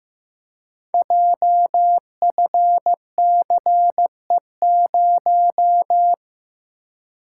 Morse code, heard as a keyed tone, about 15 wpm, 700 Hz. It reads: JFCE0
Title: Code Groups - 5 Letter:Numbers - 15wpm